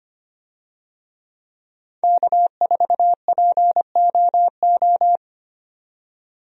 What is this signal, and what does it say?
Morse code, heard as a keyed tone, about 25 wpm, 700 Hz. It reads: K4POO
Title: USA Call Signs - 5 Characters - 25wpm